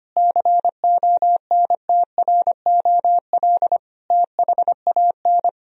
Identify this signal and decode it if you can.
Morse code, heard as a keyed tone, about 25 wpm, 700 Hz. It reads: CONTROL THAN
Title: Sentences from Top 1000 Words - Review Part 3 - 25wpm